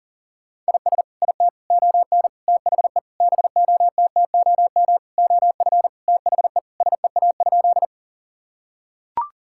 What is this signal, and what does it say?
Morse code, heard as a keyed tone, about 40 wpm, 700 Hz. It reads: IS IT ON THE BOTTOM OF THE SEA?
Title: Sentences from Top 1000 Words - Review Part 6 - 40wpm